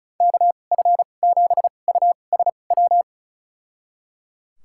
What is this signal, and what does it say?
Morse code, heard as a keyed tone, about 34 wpm, spaced as for 25 wpm, 700 Hz. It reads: KF7USW
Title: USA Call Signs - 6 Characters - 2x Character Spacing - 35wpm